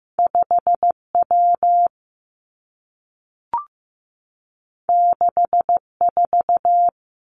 Morse code, heard as a keyed tone, about 15 wpm, 700 Hz. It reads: 5 W 6 4
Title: Two Challenging Letter-Number Combinations 15wpm